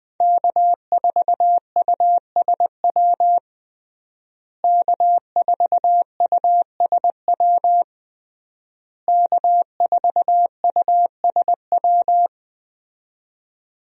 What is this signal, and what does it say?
Morse code, heard as a keyed tone, about 20 wpm, 700 Hz. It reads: K4USW K4USW K4USW
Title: USA Call Signs - 5 Characters - Repeated 3x - 20wpm